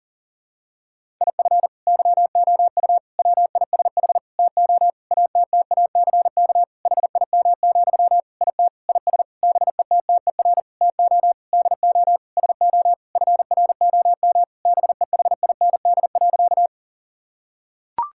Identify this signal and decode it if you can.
Morse code, heard as a keyed tone, about 37 wpm, 700 Hz. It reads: IF YOU WISH TO ATTACK HIM, IT IS BETTER TO DO SO FROM BEHIND.
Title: Sentences from Top 1600 Words 40wpm